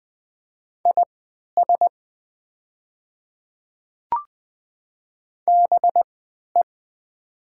Morse code, heard as a keyed tone, about 20 wpm, spaced as for 11 wpm, 700 Hz. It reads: IS BE
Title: 3x Character Spacing - Top 100 Words - 20wpm